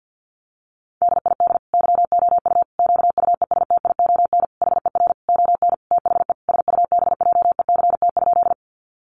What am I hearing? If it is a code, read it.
Morse code, heard as a keyed tone, about 50 wpm, 700 Hz. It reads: DID YOU QUESTION HER ON THE SUBJECT?
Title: Sentences from Top 800 Words 50wpm